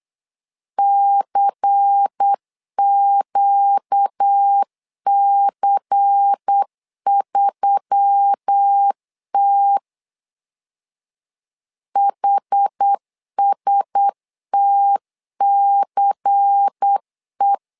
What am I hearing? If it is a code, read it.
Morse code, heard as a keyed tone, about 8 wpm, 800 Hz. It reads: CQC3T HSTCE
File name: LEZIONI / 5 LBVC / 5 LBVC 8wpm 2935